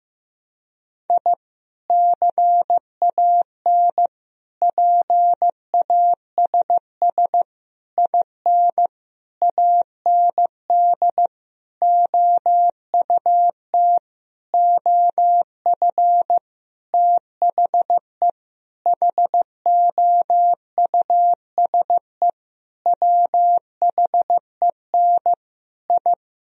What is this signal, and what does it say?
Morse code, heard as a keyed tone, about 15 wpm, 700 Hz. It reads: I CAN PASS IN AND OUT OF THE HOUSE WHEN I
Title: Sentences from Top 1000 Words - Review Part 4 - 15wpm